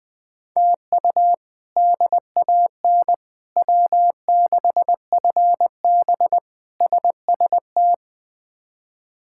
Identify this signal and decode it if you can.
Morse code, heard as a keyed tone, about 20 wpm, 700 Hz. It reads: TU DAN W6FB SST
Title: K1USN SST QSO Practice - 20wpm